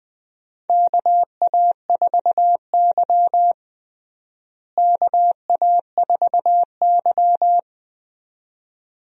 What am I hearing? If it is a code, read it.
Morse code, heard as a keyed tone, about 20 wpm, 700 Hz. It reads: KA4Y KA4Y
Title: USA Call Signs - 4 Characters - Repeated 2x - 20wpm